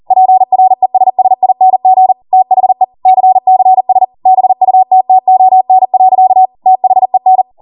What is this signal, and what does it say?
Morse code, heard as a keyed tone, about 40 wpm, 750 Hz. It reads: PRESSING THE CCS BUTTON. THEN